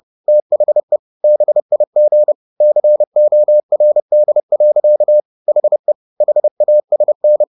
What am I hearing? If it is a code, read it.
Morse code, heard as a keyed tone, about 30 wpm, 600 Hz. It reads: THE BIG CORD. HE HASN